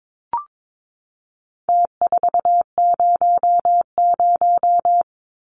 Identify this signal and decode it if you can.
Morse code, heard as a keyed tone, about 22 wpm, 700 Hz. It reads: T400
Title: Numbers - Contest Style - 4-Digit 22wpm